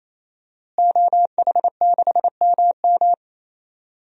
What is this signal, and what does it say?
Morse code, heard as a keyed tone, about 28 wpm, 700 Hz. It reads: OH6MM
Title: DXCC-100 Call Signs - 5 Characters - Repeated 3x - 28wpm